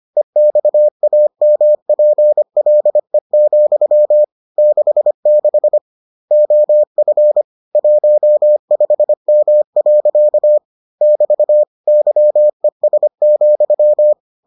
Morse code, heard as a keyed tone, about 25 wpm, 600 Hz. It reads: EXAMPLE, 66 OF 15M. <BT> YES,